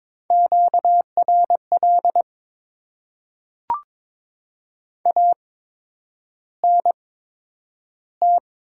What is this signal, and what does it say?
Morse code, heard as a keyed tone, about 22 wpm, spaced as for 9 wpm, 700 Hz. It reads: QRL ANT
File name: Top 20 QSO Elements - 8x Character Spacing - 22wpm